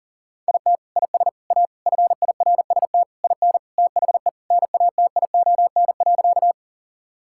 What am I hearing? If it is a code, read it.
Morse code, heard as a keyed tone, about 40 wpm, 700 Hz. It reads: IT IS A FIRST IN THE NATION.
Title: Sentences from Top 1000 Words - Review Part 6 - 40wpm